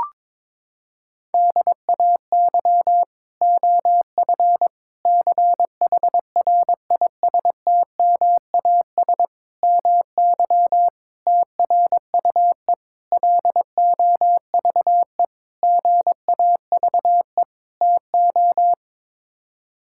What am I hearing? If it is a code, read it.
Morse code, heard as a keyed tone, about 22 wpm, 700 Hz. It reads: DAY OF CHRISTMAS MY TRUE LOVE GAVE TO
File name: practice / 22wpm / Sets of 8 Words - 22wpm